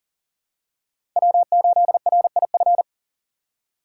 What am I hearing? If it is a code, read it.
Morse code, heard as a keyed tone, about 39 wpm, 700 Hz. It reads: W8RIF
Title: USA Call Signs - 4x Character Spacing - 40wpm